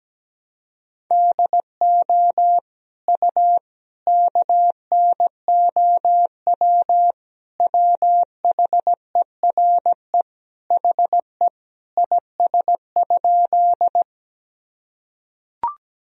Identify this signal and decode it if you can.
Morse code, heard as a keyed tone, about 17 wpm, 700 Hz. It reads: DO U KNOW WHERE HE IS?
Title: Sentences from Top 300 Words and Abbreviations 17wpm